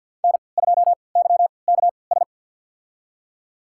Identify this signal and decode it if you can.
Morse code, heard as a keyed tone, about 53 wpm, spaced as for 29 wpm, 700 Hz. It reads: N2YKS
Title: USA Call Signs - 5 Characters - 3x Character Spacing - 50wpm